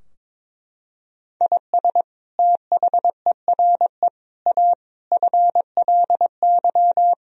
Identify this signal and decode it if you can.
Morse code, heard as a keyed tone, about 22 wpm, 700 Hz. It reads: IS THERE A FLY
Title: Sentences from Top 1000 Words - Review Part 4 - 22wpm